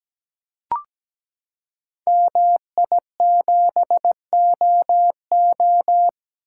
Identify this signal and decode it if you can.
Morse code, heard as a keyed tone, about 17 wpm, 700 Hz. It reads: MI7OO
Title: Code Groups - 5 Letter:Numbers - 17wpm